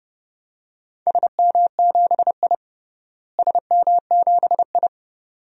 Morse code, heard as a keyed tone, about 32 wpm, 700 Hz. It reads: SM7I SM7I
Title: DXCC-100 Call Signs - 4 Characters - Repeated 2x - 30wpm